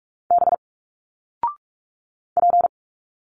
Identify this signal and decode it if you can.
Morse code, heard as a keyed tone, about 46 wpm, 700 Hz. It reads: B P
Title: Single Letters 45wpm